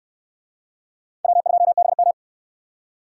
Morse code, heard as a keyed tone, about 56 wpm, 700 Hz. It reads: W2BM
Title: Speed Racing - USA 1x2 Call Signs 45wpm